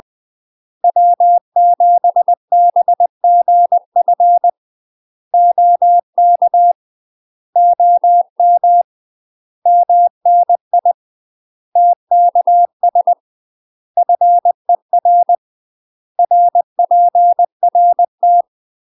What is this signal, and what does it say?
Morse code, heard as a keyed tone, about 20 wpm, 700 Hz. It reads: W7BGF OK OM MNI TKS FER RPRT
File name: UK MASTER QSOs TRAINING / 20WPM / UK MASTER QSOs 0023